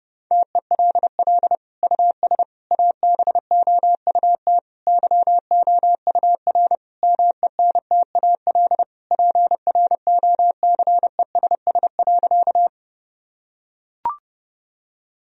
Morse code, heard as a keyed tone, about 32 wpm, 700 Hz. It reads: TELL US ABOUT YOUR MENTAL PROCESS.
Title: Sentences from Top 1000 Words - Review Part 4 - 30wpm